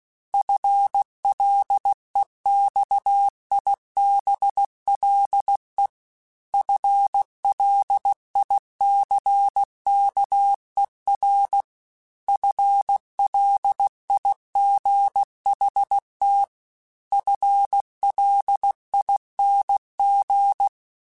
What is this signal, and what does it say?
Morse code, heard as a keyed tone, about 16 wpm, 800 Hz. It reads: FLEXIBLE FLICKER FLIGHT FLING
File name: FL Runs / FL Runs mp3s / FL Runs @15